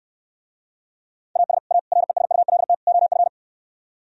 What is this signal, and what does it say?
Morse code, heard as a keyed tone, about 68 wpm, 700 Hz. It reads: AS A RESULT OF